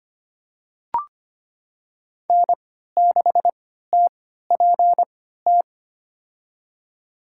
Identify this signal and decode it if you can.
Morse code, heard as a keyed tone, about 25 wpm, spaced as for 14 wpm, 700 Hz. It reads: N6TPT
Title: USA Call Signs - 5 Characters - 3x Character Spacing - 25wpm